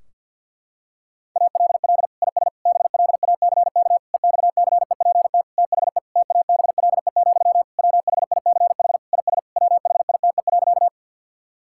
Morse code, heard as a keyed tone, about 51 wpm, 700 Hz. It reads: ALL IS BLACK EXCEPT THE TABLE, WHICH IS WHITE.
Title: Sentences from Top 1600 Words 50wpm